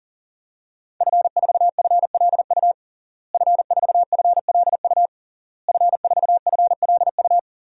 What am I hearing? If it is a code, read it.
Morse code, heard as a keyed tone, about 40 wpm, 700 Hz. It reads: F4FLU F4FLU F4FLU
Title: DXCC-100 Call Signs - 5 Characters - Repeated 3x - 40wpm